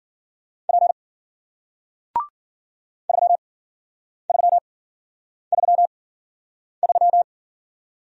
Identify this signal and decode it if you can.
Morse code, heard as a keyed tone, about 53 wpm, spaced as for 9 wpm, 700 Hz. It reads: P 3333